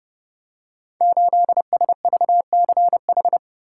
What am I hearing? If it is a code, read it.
Morse code, heard as a keyed tone, about 30 wpm, 700 Hz. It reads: 8SVCH
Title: Code Groups - 5 Letter:Numbers - 30wpm